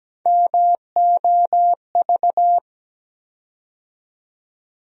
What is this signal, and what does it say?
Morse code, heard as a keyed tone, about 17 wpm, 700 Hz. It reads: MOV
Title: USA Call Sign Postfixes - 3 Letters 17wpm